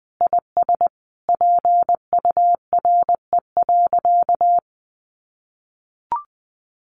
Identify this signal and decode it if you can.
Morse code, heard as a keyed tone, about 20 wpm, 700 Hz. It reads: IS PURE.
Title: Sentences from Top 1500 Words 20wpm